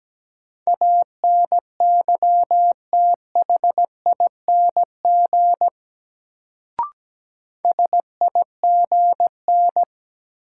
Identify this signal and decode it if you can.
Morse code, heard as a keyed tone, about 17 wpm, 700 Hz. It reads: ANYTHING SIGN
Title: Sentences from Top 1000 Words - New Word Drill - 17wpm